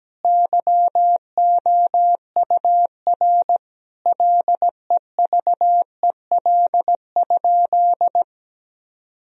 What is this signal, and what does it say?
Morse code, heard as a keyed tone, about 17 wpm, 700 Hz. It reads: YOUR LEVEL?
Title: Sentences from Top 1000 Words - Review Part 4 - 17wpm